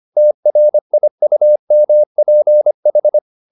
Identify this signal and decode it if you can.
Morse code, heard as a keyed tone, about 25 wpm, 600 Hz. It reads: TRIUMPH